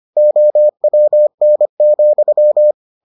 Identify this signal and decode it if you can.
Morse code, heard as a keyed tone, about 25 wpm, 600 Hz. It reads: OWN,